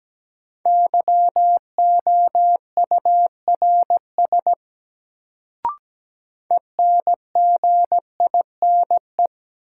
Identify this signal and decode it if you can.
Morse code, heard as a keyed tone, about 17 wpm, 700 Hz. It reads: YOURS ENGINE
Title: Sentences from Top 1500 Words - New Word Drill - 17wpm